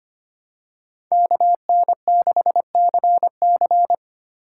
Morse code, heard as a keyed tone, about 25 wpm, 700 Hz. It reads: KN6CC
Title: USA Call Signs - 5 Characters - 2x Character Spacing - 25wpm